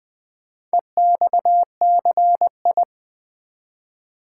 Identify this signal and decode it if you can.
Morse code, heard as a keyed tone, about 20 wpm, 700 Hz. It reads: EXCI
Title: Build A Word - 4 Letters At A Time - 10-Letter Words - 20wpm